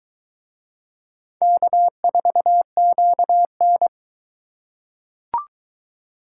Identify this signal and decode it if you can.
Morse code, heard as a keyed tone, about 21 wpm, 700 Hz. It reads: K4QN